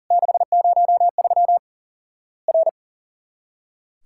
Morse code, heard as a keyed tone, about 40 wpm, 700 Hz. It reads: 603 R
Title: Worked All Europe DX Contest - QTC Exchange - 40wpm